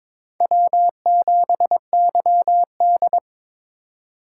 Morse code, heard as a keyed tone, about 26 wpm, 700 Hz. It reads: W7YD
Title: Speed Racing - USA 1x2 Call Signs 22wpm